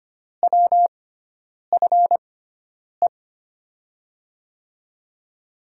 Morse code, heard as a keyed tone, about 25 wpm, spaced as for 9 wpm, 700 Hz. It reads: WFE
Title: USA Call Signs - 6 Characters - 6x Character Spacing - 25wpm